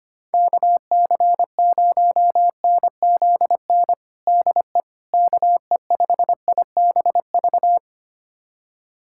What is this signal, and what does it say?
Morse code, heard as a keyed tone, about 25 wpm, 700 Hz. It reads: KC0NZN DE KE5IBV
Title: USA Callsign DE Callsign 25wpm